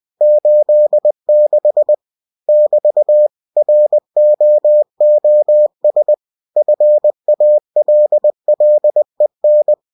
Morse code, heard as a keyed tone, about 20 wpm, 600 Hz. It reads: 86 <BT> ROOS FALLEN